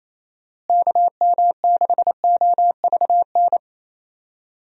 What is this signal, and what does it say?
Morse code, heard as a keyed tone, about 28 wpm, 700 Hz. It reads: KM6OVN
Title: USA Call Signs - 6 Characters - Repeated 2x - 28wpm